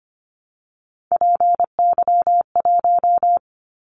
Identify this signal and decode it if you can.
Morse code, heard as a keyed tone, about 25 wpm, 700 Hz. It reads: PY1
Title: Callsign Prefixes for Top 100 DXCC - 25wpm